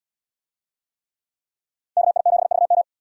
Morse code, heard as a keyed tone, about 78 wpm, 700 Hz. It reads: KI7VM